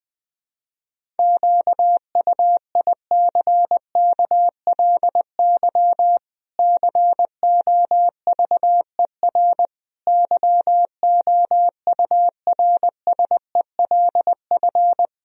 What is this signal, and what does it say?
Morse code, heard as a keyed tone, about 20 wpm, 700 Hz. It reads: QUICKLY COVER YOURSELF
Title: Sentences from Top 1000 Words - Review Part 1 - 20wpm